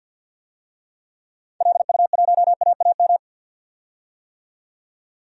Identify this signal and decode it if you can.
Morse code, heard as a keyed tone, about 46 wpm, 700 Hz. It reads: LU1AAM
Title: DXCC-100 Call Signs - 6 Characters - 50wpm